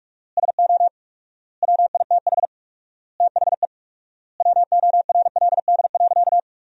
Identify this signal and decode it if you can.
Morse code, heard as a keyed tone, about 46 wpm, 700 Hz. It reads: SO WITH THE WORLD.
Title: 4x Word-Spacing - Sentences from Top 200 Words - 45wpm